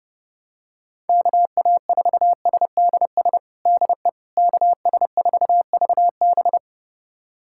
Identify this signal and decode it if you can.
Morse code, heard as a keyed tone, about 30 wpm, 700 Hz. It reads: KA4SDS DE KS4VB
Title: USA Callsign DE Callsign 30wpm